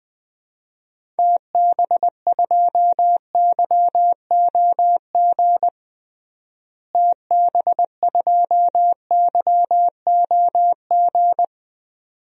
Morse code, heard as a keyed tone, about 21 wpm, 700 Hz. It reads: TB2YOG TB2YOG
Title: DXCC-100 Call Signs - 6 Characters - Repeated 2x - 20wpm